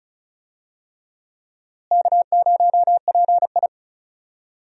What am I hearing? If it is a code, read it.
Morse code, heard as a keyed tone, about 36 wpm, 700 Hz. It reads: K0PI
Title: USA Call Signs - 4 Characters - Repeated 3x - 35wpm